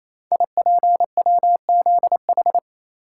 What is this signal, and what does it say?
Morse code, heard as a keyed tone, about 28 wpm, 690 Hz. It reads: IPWZH
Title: Code Groups - 5 Letters - 28wpm